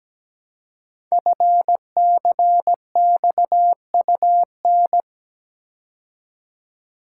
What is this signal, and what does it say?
Morse code, heard as a keyed tone, about 17 wpm, 700 Hz. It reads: FCXUN
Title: Code Groups - 5 Letters - 17wpm